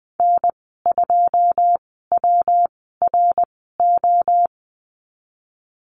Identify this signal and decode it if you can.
Morse code, heard as a keyed tone, about 20 wpm, spaced as for 15 wpm, 700 Hz. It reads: N2WRO
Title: 128 Lesson 21g - Number 6 - Call Signs 20wpm